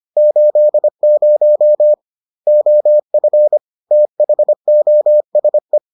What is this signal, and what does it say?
Morse code, heard as a keyed tone, about 25 wpm, 600 Hz. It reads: 80 OF THOSE